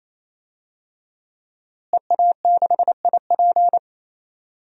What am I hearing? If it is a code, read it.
Morse code, heard as a keyed tone, about 28 wpm, 700 Hz. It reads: EA6IP
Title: DXCC-100 Call Signs - 5 Characters - Repeated 3x - 28wpm